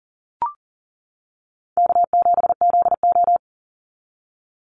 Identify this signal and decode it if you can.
Morse code, heard as a keyed tone, about 40 wpm, 700 Hz. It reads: K7ZO
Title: NA SOTA Chaser Callsigns 40wpm